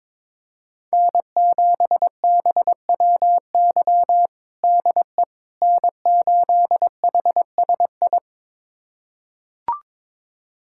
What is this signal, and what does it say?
Morse code, heard as a keyed tone, about 22 wpm, 700 Hz. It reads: N7BWY DE N8HSI
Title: USA Callsign DE Callsign - 5 Characters - 22wpm